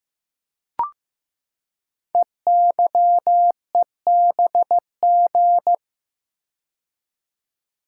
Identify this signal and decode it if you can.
Morse code, heard as a keyed tone, about 15 wpm, 700 Hz. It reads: EYEBG
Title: Code Groups - 5 Letters - 15wpm